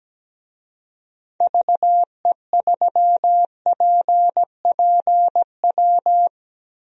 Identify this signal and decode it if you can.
Morse code, heard as a keyed tone, about 17 wpm, 700 Hz. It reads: VE3PPW
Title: DXCC-100 Call Signs - 6 Characters - 17wpm